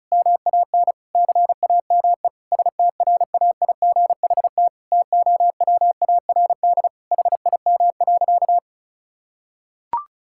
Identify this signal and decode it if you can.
Morse code, heard as a keyed tone, about 36 wpm, 700 Hz. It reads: MAN CAME STRAIGHT TOWARD HIM.
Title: Sentences from Top 1000 Words - Review Part 6 - 35wpm